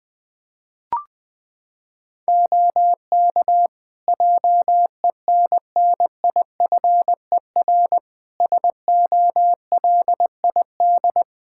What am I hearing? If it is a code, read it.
Morse code, heard as a keyed tone, about 20 wpm, 700 Hz. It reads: OK JENNIFER SOLID
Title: Practice QSO - Exchange 3 - More Info - 20wpm